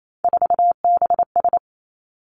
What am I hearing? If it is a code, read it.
Morse code, heard as a keyed tone, about 28 wpm, 690 Hz. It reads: 4BS
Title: Three Challenging Letter-Number Combinations 28wpm